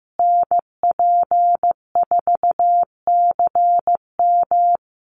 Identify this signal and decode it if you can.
Morse code, heard as a keyed tone, about 15 wpm, 700 Hz. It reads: NP4CM
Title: USA Call Signs Repeated 3x 15wpm